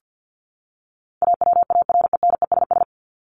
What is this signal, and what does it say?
Morse code, heard as a keyed tone, about 50 wpm, 700 Hz. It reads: AWARENESS
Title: Top 1501-1750 Words 50wpm